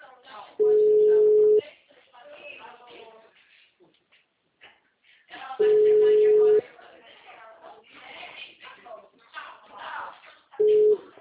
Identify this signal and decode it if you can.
Ringback tone (Europe, 425 Hz), with speech.